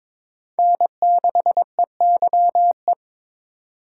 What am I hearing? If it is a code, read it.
Morse code, heard as a keyed tone, about 21 wpm, 700 Hz. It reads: N6EYE
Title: USA Call Signs - 4x Character Spacing - 22wpm